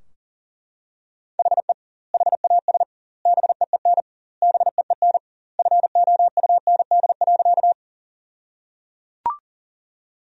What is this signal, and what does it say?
Morse code, heard as a keyed tone, about 37 wpm, 700 Hz. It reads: HE HAS BEEN BEEN FOUND.
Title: 2x Word-Spacing - Sentences from Top 300 Words - 40wpm